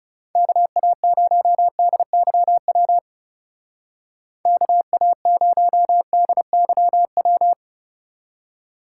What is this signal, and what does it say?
Morse code, heard as a keyed tone, about 37 wpm, 700 Hz. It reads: KA0DYW KA0DYW
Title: Speed Racing - USA Call Signs 30wpm